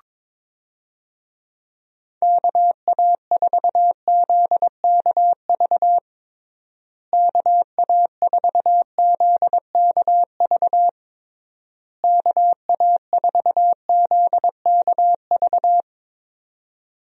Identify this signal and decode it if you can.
Morse code, heard as a keyed tone, about 22 wpm, 700 Hz. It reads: KA4ZKV KA4ZKV KA4ZKV
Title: USA Call Signs - 6 Characters - Repeated 3x - 22wpm